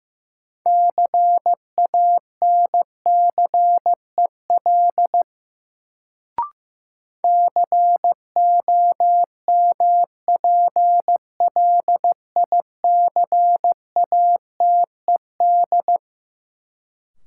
Morse code, heard as a keyed tone, about 15 wpm, 700 Hz. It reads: CANCEL COMPLICATED
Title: Sentences from Top 1200 Words - New Word Drill Alt - 15wpm